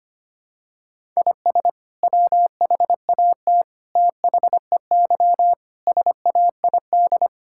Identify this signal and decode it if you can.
Morse code, heard as a keyed tone, about 25 wpm, 700 Hz. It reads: IS WHAT THEY SAID
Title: Sentences from Top 1000 Words - Review Part 5 - 25wpm